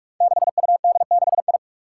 Morse code, heard as a keyed tone, about 44 wpm, 700 Hz. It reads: 6UD6I
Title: Five Challenging Letter-Number Combinations 45wpm